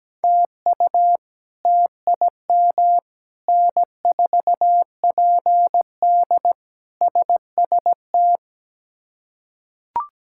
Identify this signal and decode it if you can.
Morse code, heard as a keyed tone, about 17 wpm, 700 Hz. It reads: TU TIM N4PD SST
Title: K1USN SST QSO Practice - 17wpm